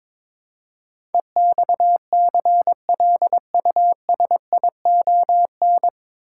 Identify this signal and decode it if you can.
Morse code, heard as a keyed tone, about 22 wpm, 700 Hz. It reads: EXCLUSION
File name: 9 Letter Words - First 3 Letters Given - 22wpm